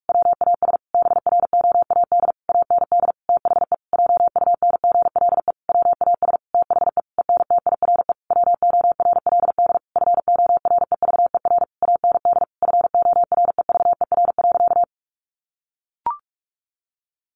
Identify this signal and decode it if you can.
Morse code, heard as a keyed tone, about 44 wpm, 700 Hz. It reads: WAS BROAD AND THE JUNGLE WAS THE ENTIRE WORLD FOREVER AND FOREVER.
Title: A Sound Of Thunder 45wpm